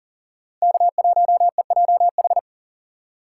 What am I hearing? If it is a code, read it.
Morse code, heard as a keyed tone, about 38 wpm, 700 Hz. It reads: K1EJH
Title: USA Call Signs - 5 Characters - 4x Character Spacing - 40wpm